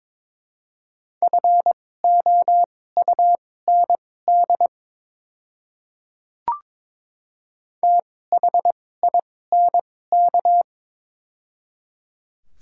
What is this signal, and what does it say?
Morse code, heard as a keyed tone, about 22 wpm, spaced as for 16 wpm, 700 Hz. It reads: FOUND THINK
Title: 2x Character Spacing - Top 300 Words - 22wpm